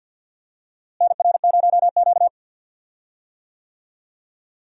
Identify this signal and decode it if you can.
Morse code, heard as a keyed tone, about 48 wpm, 700 Hz. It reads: NR0Q
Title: USA Call Signs - 4 Characters - Repeated 2x - 50wpm